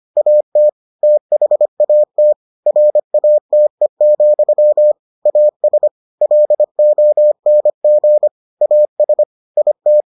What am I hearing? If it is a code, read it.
Morse code, heard as a keyed tone, about 25 wpm, 600 Hz. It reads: AT THAT RATE, AS LONG AS IT